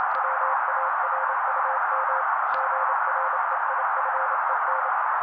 Morse code, heard as a keyed tone, about 28 wpm, 530 Hz. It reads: WARUM GREIFEN
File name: daily / 27wpm noise